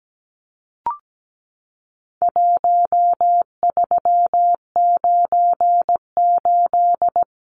Morse code, heard as a keyed tone, about 17 wpm, 700 Hz. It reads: 1398
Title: Numbers - Contest Style - 4-Digit 17wpm